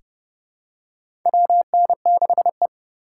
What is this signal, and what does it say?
Morse code, heard as a keyed tone, about 30 wpm, 700 Hz. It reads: WN6E
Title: USA Call Signs - 4 Characters - 4x Character Spacing - 30wpm